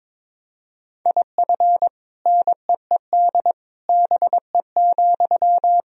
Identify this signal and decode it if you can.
Morse code, heard as a keyed tone, about 22 wpm, 700 Hz. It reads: IF NEED BE,
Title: Sentences from Top 1000 Words - Review Part 1 - 22wpm